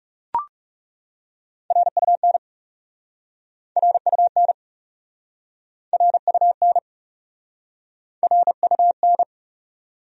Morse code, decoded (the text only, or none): RUN RUN RUN RUN